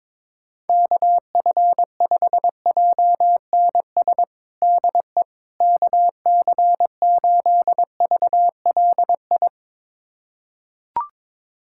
Morse code, heard as a keyed tone, about 22 wpm, 700 Hz. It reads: KF5JNS DE KC8VLI
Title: USA Callsign DE Callsign - 6 Characters - 22wpm